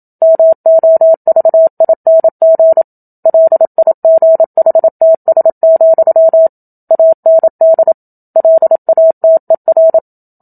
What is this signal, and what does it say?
Morse code, heard as a keyed tone, about 27 wpm, 650 Hz. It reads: MOVING LIGHTS, AND LATER
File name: Book Two / 01-Chapter 1 - 27 WPM